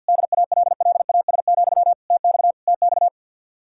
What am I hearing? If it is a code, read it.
Morse code, heard as a keyed tone, about 60 wpm, 700 Hz. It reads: DALLAS, TX TX
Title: Speed-Racing Common QSO Phrases Encore - 50wpm